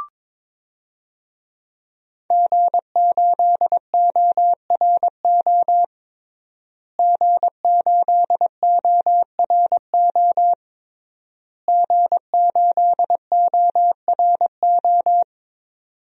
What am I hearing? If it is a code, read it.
Morse code, heard as a keyed tone, about 22 wpm, 700 Hz. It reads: G8ORO G8ORO G8ORO
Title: DXCC-100 Call Signs - 5 Characters - Repeated 3x - 22wpm